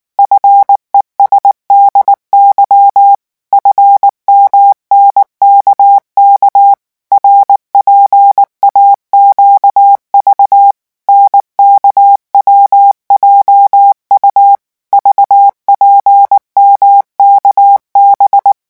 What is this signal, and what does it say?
Morse code, heard as a keyed tone, about 19 wpm, 800 Hz. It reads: FESDY FMNKK RPAQV NKWJU VPMKB
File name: morse mp3 / revision7-vitesse 18 mots